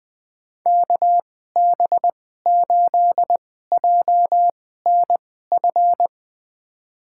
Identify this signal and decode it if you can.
Morse code, heard as a keyed tone, about 20 wpm, spaced as for 14 wpm, 700 Hz. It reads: KB8JNF
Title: USA Call Signs - 6 Characters - 2x Character Spacing - 20wpm